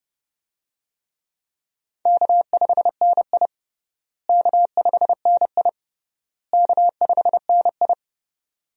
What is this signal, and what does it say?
Morse code, heard as a keyed tone, about 30 wpm, 700 Hz. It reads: K5NI K5NI K5NI
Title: USA Call Signs - 4 Characters - Repeated 3x - 30wpm